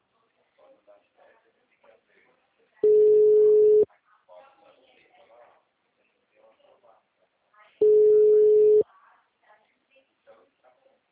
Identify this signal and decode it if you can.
Ringback tone (Europe, 425 Hz), with speech.